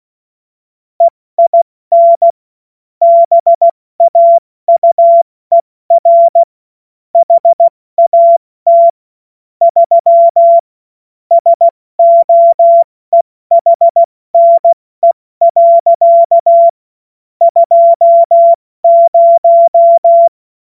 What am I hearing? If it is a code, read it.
Morse code, heard as a keyed tone, about 16 wpm, spaced as for 14 wpm, 680 Hz. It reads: EIN BAUER HAT 3 SOEHNE. 20
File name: Witz-08-Juli-2025-70-F80